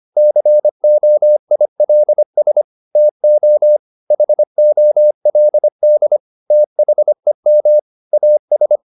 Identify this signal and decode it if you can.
Morse code, heard as a keyed tone, about 25 wpm, 600 Hz. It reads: COILS TO HOLD THEM AS
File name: the old man and the sea part 1 25wpm